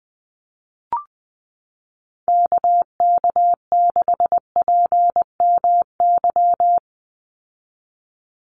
Morse code, heard as a keyed tone, about 20 wpm, 700 Hz. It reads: KK6PMY
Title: USA Call Signs 20wpm